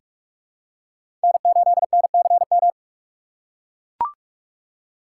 Morse code, heard as a keyed tone, about 43 wpm, 700 Hz. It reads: N8NCM
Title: USA Call Signs - 5 Characters - 3x Character Spacing - 45wpm